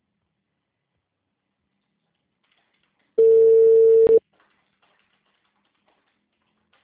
Ringback tone (China, 450 Hz).